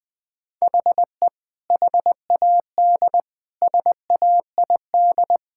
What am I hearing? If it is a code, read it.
Morse code, heard as a keyed tone, about 21 wpm, 700 Hz. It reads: HE HAD SAID
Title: Sentences from Top 1600 Words 20wpm